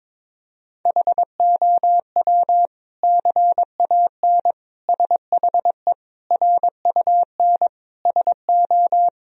Morse code, heard as a keyed tone, about 22 wpm, 700 Hz. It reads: HOW CAN SHE RUN SO
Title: Sentences from Top 1000 Words - Review Part 2 - 22wpm